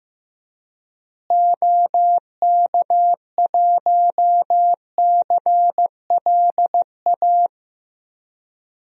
Morse code, heard as a keyed tone, about 16 wpm, 700 Hz. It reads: OK1CLA
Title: DXCC-100 Call Signs - 6 Characters - Repeated 3x - 15wpm